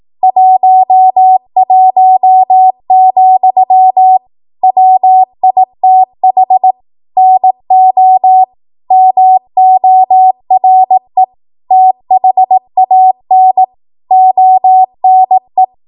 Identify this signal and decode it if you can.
Morse code, heard as a keyed tone, about 18 wpm, 750 Hz. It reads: 11, WITH NO MORE THAN ONE
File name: Morse / 18WPM